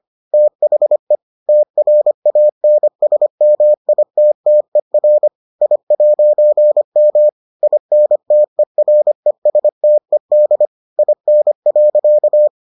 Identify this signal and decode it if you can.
Morse code, heard as a keyed tone, about 25 wpm, 600 Hz. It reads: THE TRANSMITTER I'M INTERESTED IN.